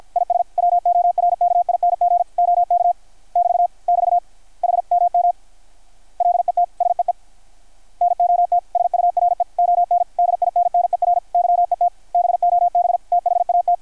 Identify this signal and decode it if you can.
Morse code, heard as a keyed tone, about 53 wpm, 700 Hz. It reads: IS WORKING OK <BT> <BT> HMM LET SEE NOT SURE ON DINNER YET BOB THAT
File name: archived / CRSnet